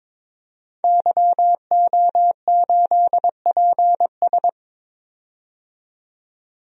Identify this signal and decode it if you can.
Morse code, heard as a keyed tone, about 22 wpm, 700 Hz. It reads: YO8PS
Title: DXCC-100 Call Signs - 5 Characters - 22wpm